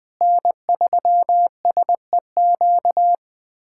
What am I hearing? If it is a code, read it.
Morse code, heard as a keyed tone, about 21 wpm, 700 Hz. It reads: N3SEQ
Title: USA Callsign DE Callsign 20wpm